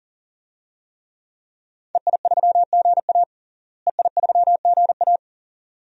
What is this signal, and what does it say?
Morse code, heard as a keyed tone, about 40 wpm, 700 Hz. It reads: EI3GA EI3GA
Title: DXCC-100 Call Signs - 5 Characters - Repeated 2x - 40wpm